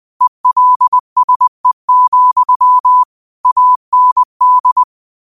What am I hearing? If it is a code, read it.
Morse code, heard as a keyed tone, about 20 wpm, 1000 Hz. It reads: ELSE, AND
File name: qotdfiles / 20wpm